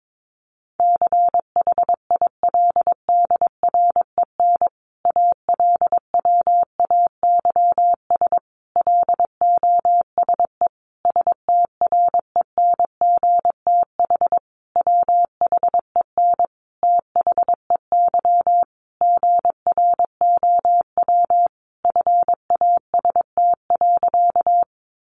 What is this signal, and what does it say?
Morse code, heard as a keyed tone, about 22 wpm, 700 Hz. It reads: CHILDREN ALWAYS LOSE STRENGTH WHEN THEY GROW FAST.
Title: Sentences from Top 1300 Words 22wpm